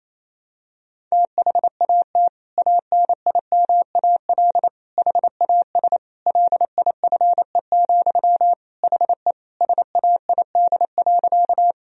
Morse code, heard as a keyed tone, about 28 wpm, 700 Hz. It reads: THAT ANIMAL HAS LIFE, HE SAID.
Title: Sentences from Top 1000 Words - Review Part 1 - 28wpm